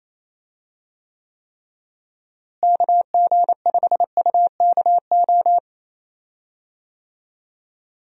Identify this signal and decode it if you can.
Morse code, heard as a keyed tone, about 28 wpm, 700 Hz. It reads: KG5UKO
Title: USA Call Signs - 6 Characters - Repeated 3x - 28wpm